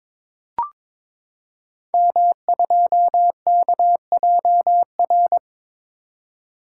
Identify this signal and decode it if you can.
Morse code, heard as a keyed tone, about 22 wpm, 690 Hz. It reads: M2KJR
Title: Code Groups - 5 Letter:Numbers - 22wpm